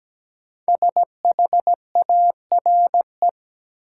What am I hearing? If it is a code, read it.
Morse code, heard as a keyed tone, about 17 wpm, 700 Hz. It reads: SHARE
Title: Sentences from Top 900 Words - New Word Drill - 17wpm